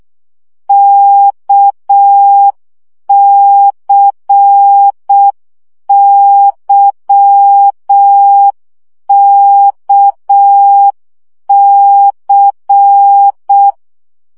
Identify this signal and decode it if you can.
Morse code, heard as a keyed tone, about 6 wpm, 800 Hz. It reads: KCYKC
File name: CW gyakorlas k c y 5 baud